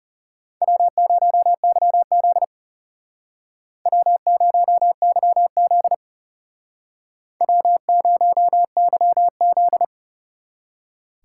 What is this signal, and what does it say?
Morse code, heard as a keyed tone, about 36 wpm, 700 Hz. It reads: W0YZ W0YZ W0YZ
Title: Speed Racing - USA 1x2 Call Signs 30wpm